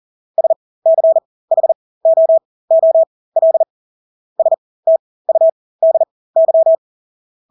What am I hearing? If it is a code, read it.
Morse code, heard as a keyed tone, about 40 wpm, spaced as for 20 wpm, 650 Hz. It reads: SCHOOL STUDY
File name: Engelske ord 40-20 wpm